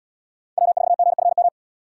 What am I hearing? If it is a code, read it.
Morse code, heard as a keyed tone, about 78 wpm, 700 Hz. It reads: W4GFM